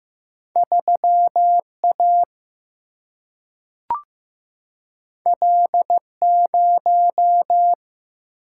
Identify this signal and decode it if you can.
Morse code, heard as a keyed tone, about 15 wpm, 700 Hz. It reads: 3A L0
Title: Single-Letter Single-Number 15wpm